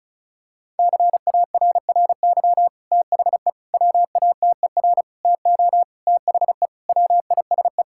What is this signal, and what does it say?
Morse code, heard as a keyed tone, about 34 wpm, 700 Hz. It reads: CARRY THE WATER TO THE WISE
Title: Sentences from Top 1600 Words 35wpm